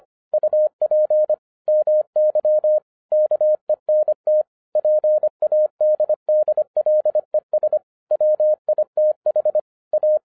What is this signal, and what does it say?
Morse code, heard as a keyed tone, about 25 wpm, 610 Hz. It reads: UP MY KENT PADDLES WITH A
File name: FIRST FISTS CONTACT-25wpm